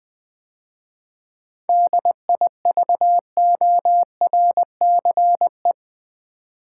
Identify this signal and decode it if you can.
Morse code, heard as a keyed tone, about 20 wpm, 700 Hz. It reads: DIVORCE
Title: 7 Letter Words - First 3 Letters Given - 20wpm